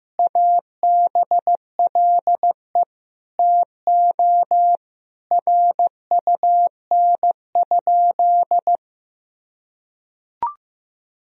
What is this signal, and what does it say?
Morse code, heard as a keyed tone, about 15 wpm, 700 Hz. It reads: ABLE TO RUN?
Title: Sentences from Top 1000 Words - Review Part 2 - 15wpm